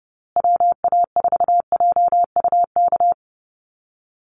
Morse code, heard as a keyed tone, about 30 wpm, 700 Hz. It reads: WA4JUK
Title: Call Signs Repeated 3x - 30wpm